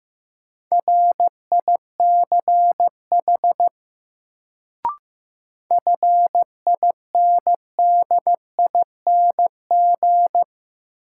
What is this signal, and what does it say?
Morse code, heard as a keyed tone, about 15 wpm, 700 Hz. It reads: RICH FINDING
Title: Sentences from Top 1100 Words - New Word Drill - 15wpm